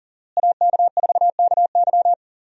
Morse code, heard as a keyed tone, about 40 wpm, 700 Hz. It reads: AK4KY
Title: USA Call Signs - 5 Characters - Repeated 3x - 40wpm